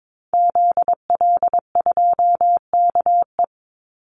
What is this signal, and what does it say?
Morse code, heard as a keyed tone, about 22 wpm, 700 Hz. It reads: ZL2KE
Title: Call Signs Repeated 2x - 22wpm